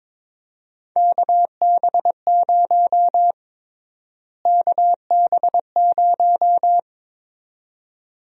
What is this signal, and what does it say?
Morse code, heard as a keyed tone, about 22 wpm, 700 Hz. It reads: KB0 KB0
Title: USA Call Sign Prefixes - Repeated 2x - 22wpm